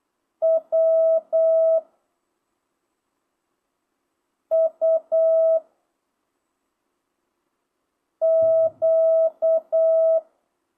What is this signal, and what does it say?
Morse code, heard as a keyed tone, about 8 wpm, 660 Hz. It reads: W U Q
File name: jh8cbh / 2025cw